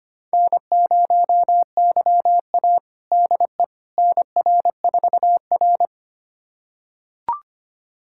Morse code, heard as a keyed tone, about 25 wpm, 700 Hz. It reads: N0YA DE NR4R
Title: USA Callsign DE Callsign - 4 Characters - 25wpm